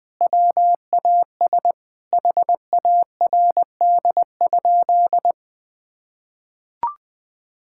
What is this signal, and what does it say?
Morse code, heard as a keyed tone, about 20 wpm, 700 Hz. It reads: WAS HARD?
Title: Sentences from Top 1000 Words - Review Part 5 - 20wpm